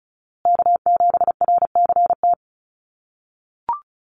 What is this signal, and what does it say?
Morse code, heard as a keyed tone, about 34 wpm, 700 Hz. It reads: K7RCT
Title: USA Call Signs Repeated 3x 35wpm